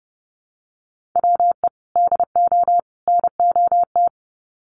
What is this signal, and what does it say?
Morse code, heard as a keyed tone, about 30 wpm, 700 Hz. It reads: WE DO NOT
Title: Sets of 3 Words 30wpm